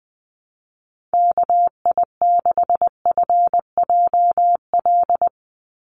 Morse code, heard as a keyed tone, about 19 wpm, 700 Hz. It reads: KI6FJL
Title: USA Call Signs 20wpm